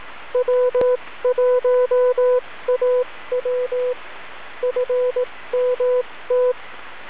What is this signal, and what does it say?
Morse code, heard as a keyed tone, about 18 wpm, spaced as for 15 wpm, 500 Hz. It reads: W1AW FMT
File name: w1aw-fall-80m-snip